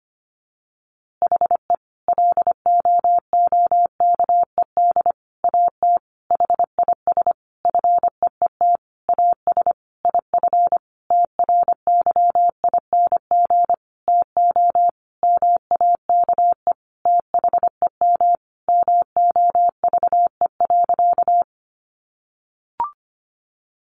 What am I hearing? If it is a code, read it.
Morse code, heard as a keyed tone, about 25 wpm, 700 Hz. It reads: HE LOOKED AT HIS FEET AS IF TRYING TO MAKE THEM MOVE.
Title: A Sound Of Thunder 25wpm